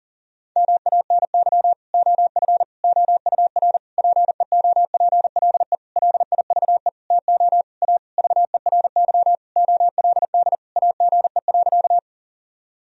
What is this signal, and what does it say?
Morse code, heard as a keyed tone, about 40 wpm, 700 Hz. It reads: MANY OF OUR PEOPLE LIVE TO A VERY OLD AGE.
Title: Sentences from Top 1000 Words - Review Part 3 - 40wpm